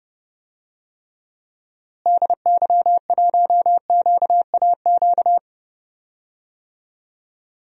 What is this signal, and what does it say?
Morse code, heard as a keyed tone, about 31 wpm, 700 Hz. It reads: DY1QAQ
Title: DXCC-100 Call Signs - 6 Characters - Repeated 2x - 30wpm